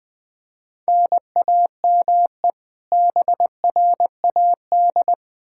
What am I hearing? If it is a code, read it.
Morse code, heard as a keyed tone, about 21 wpm, 700 Hz. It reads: NAME BRAD
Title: Practice QSO - Exchange 1 - Beginning - 20wpm